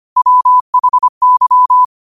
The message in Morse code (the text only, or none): WHY